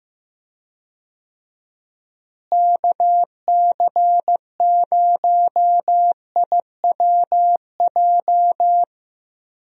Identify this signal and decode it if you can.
Morse code, heard as a keyed tone, about 15 wpm, 700 Hz. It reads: KC0IWJ
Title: USA Call Signs - 6 Characters - Repeated 3x - 15wpm